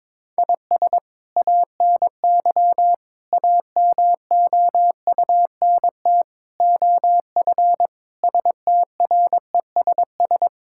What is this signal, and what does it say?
Morse code, heard as a keyed tone, about 22 wpm, 700 Hz. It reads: IS ANY AMOUNT OF STRESS
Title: Sentences from Top 1000 Words - Review Part 6 - 22wpm